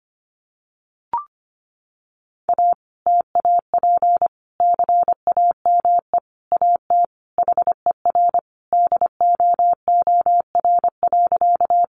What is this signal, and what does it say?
Morse code, heard as a keyed tone, about 25 wpm, 700 Hz. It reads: A TAP CAME AT HER DOOR.
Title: Sentences from Top 1400 Words 25wpm